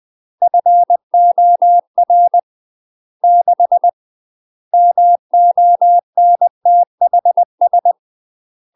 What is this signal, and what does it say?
Morse code, heard as a keyed tone, about 20 wpm, 700 Hz. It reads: FOR 6 MONTHS
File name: UK MASTER QSOs TRAINING / 20WPM / UK MASTER QSOs 0032